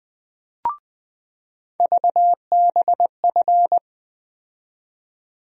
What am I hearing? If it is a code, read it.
Morse code, heard as a keyed tone, about 20 wpm, 700 Hz. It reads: VBF
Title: USA Call Sign Postfixes - 3 Letters 20wpm